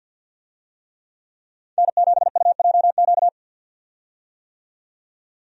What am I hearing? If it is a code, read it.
Morse code, heard as a keyed tone, about 53 wpm, 700 Hz. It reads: N7UJQ
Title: USA Call Signs - 5 Characters - 50wpm